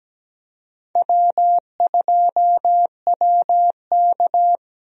Morse code, heard as a keyed tone, about 17 wpm, 700 Hz. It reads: W2WK
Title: USA Call Signs - 4 Characters - Repeated 3x - 17wpm